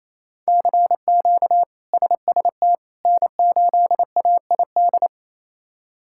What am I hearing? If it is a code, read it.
Morse code, heard as a keyed tone, about 28 wpm, 700 Hz. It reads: CQ SST N8AID
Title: K1USN SST QSO Practice - 28wpm